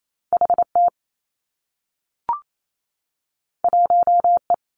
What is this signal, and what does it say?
Morse code, heard as a keyed tone, about 28 wpm, 700 Hz. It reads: H T 1 E
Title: Two Challenging Letter-Number Combinations 28wpm